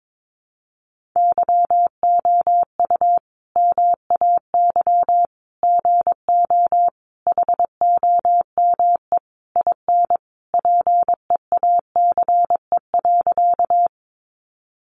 Morse code, heard as a keyed tone, about 22 wpm, 700 Hz. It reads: YOU MAY GO HOME IN PEACE.
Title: Sentences from Top 1500 Words 22wpm